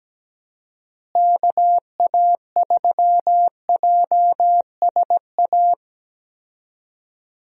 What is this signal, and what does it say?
Morse code, heard as a keyed tone, about 17 wpm, 700 Hz. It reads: KA3JSA
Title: USA Call Signs - 6 Characters - Repeated 3x - 17wpm